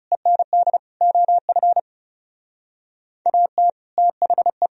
Morse code, heard as a keyed tone, about 36 wpm, 700 Hz. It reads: END OF AT THE
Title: Speed-Racing - Sets of 5 Words 30wpm